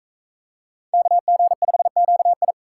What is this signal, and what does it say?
Morse code, heard as a keyed tone, about 44 wpm, 700 Hz. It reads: KG5QI